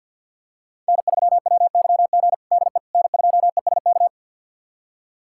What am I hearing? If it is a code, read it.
Morse code, heard as a keyed tone, about 52 wpm, 700 Hz. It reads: N3WYG DE N2ESK
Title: USA Callsign DE Callsign - 5 Characters - 50wpm